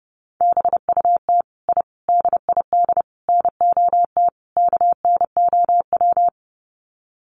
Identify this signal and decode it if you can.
Morse code, heard as a keyed tone, about 30 wpm, 700 Hz. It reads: BUT I DID NOT KNOW
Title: Sets of 5 Words Encore 30wpm